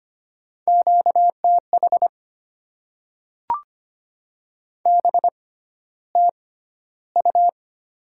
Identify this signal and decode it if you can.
Morse code, heard as a keyed tone, about 25 wpm, spaced as for 13 wpm, 700 Hz. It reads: QTH BTU
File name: Top 20 QSO Elements - 6x Character Spacing - 25wpm